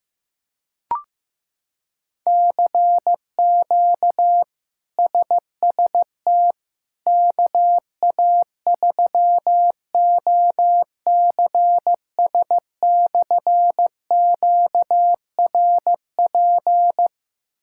Morse code, heard as a keyed tone, about 15 wpm, 700 Hz. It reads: CQ SST KA3OCS/QRP
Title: K1USN SST QSO Practice - 15wpm